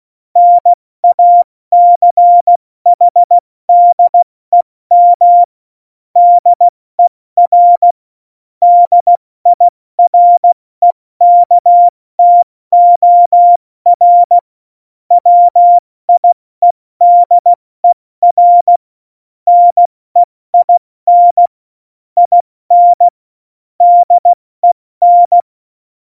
Morse code, heard as a keyed tone, about 16 wpm, 700 Hz. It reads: NACHDEM DER DIREKTOR WIEDER NEIN IN DEN
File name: Morsekurs / Witze / Witz 02. Oktober 2018, 70 F80